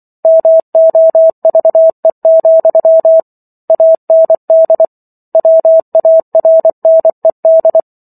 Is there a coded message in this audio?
Yes: Morse code from 0.2 to 7.8 s, heard as a keyed tone, about 24 wpm, 650 Hz. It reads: MOVE, AND WARNED